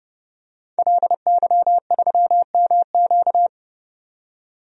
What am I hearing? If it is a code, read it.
Morse code, heard as a keyed tone, about 31 wpm, 700 Hz. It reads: LY3MQ
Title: DXCC-100 Call Signs - 5 Characters - 30wpm